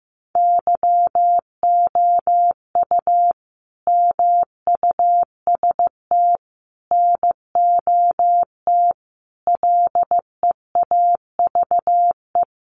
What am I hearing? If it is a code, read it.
Morse code, heard as a keyed tone, about 15 wpm, 700 Hz. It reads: YOU MUST NOT LEAVE
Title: Sentences from Top 600 Words 15wpm